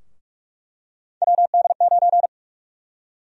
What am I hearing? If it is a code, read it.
Morse code, heard as a keyed tone, about 42 wpm, 700 Hz. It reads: WD9
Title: USA Call Sign Prefixes - Repeated 2x - 45wpm